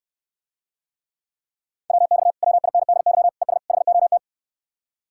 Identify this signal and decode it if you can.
Morse code, heard as a keyed tone, about 69 wpm, 700 Hz. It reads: WX WINDY ES HOT